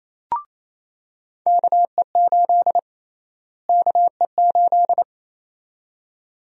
Morse code, heard as a keyed tone, about 28 wpm, 700 Hz. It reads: KE8 KE8
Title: USA Call Sign Prefixes - Repeated 2x - 28wpm